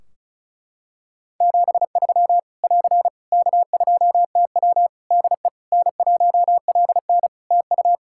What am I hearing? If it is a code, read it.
Morse code, heard as a keyed tone, about 36 wpm, 700 Hz. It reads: 73 <AR> K2TW DE N1LN TU
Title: Practice QSO - Exchange 4 - Ending - 35wpm